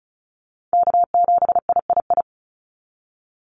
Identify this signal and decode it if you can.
Morse code, heard as a keyed tone, about 34 wpm, 700 Hz. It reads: K7III
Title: NA SOTA Chaser Callsigns 35wpm